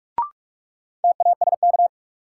Morse code, heard as a keyed tone, about 42 wpm, 700 Hz. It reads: TASK
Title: Sentences from Top 1100 Words - New Word Drill Alt - 45wpm